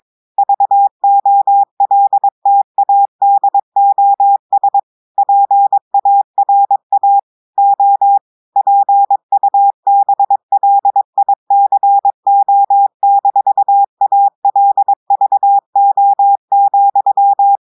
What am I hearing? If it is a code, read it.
Morse code, heard as a keyed tone, about 22 wpm, 800 Hz. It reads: VOLTADOS PARA O PUBLICO-ALVO,
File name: audios / portugais / portugais texte en clair 2 vitesse 22 mots minutes